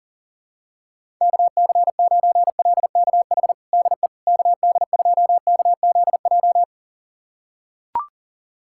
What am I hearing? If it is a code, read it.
Morse code, heard as a keyed tone, about 40 wpm, 700 Hz. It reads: KC9LKH DE KD2KZJ
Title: USA Callsign DE Callsign 40wpm